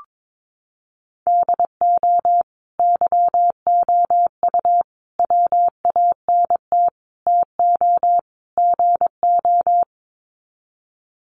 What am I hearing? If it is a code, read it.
Morse code, heard as a keyed tone, about 22 wpm, 700 Hz. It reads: DO YOU WANT TO GO
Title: Sets of 5 Words Encore2 22wpm